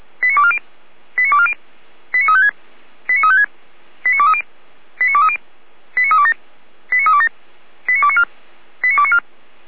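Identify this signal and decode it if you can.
Five-tone selective call (ZVEI), heard as quick runs of five tones, tones of about 70 ms. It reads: 89240 89240 89356 89356 89230 89230 89327 89327 89294 89294
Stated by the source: ZVEI five-tone call received by SDR, from the Feuerwehr Orlamünde audio server.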